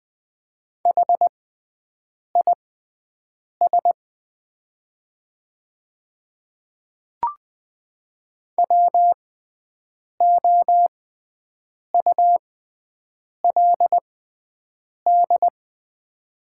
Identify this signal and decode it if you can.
Morse code, heard as a keyed tone, about 20 wpm, spaced as for 7 wpm, 690 Hz. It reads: HIS WOULD
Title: 6x Character Spacing - Top 100 Words - 20wpm